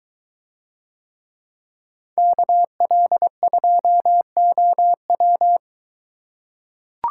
Morse code, heard as a keyed tone, about 22 wpm, 700 Hz. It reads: KL2OW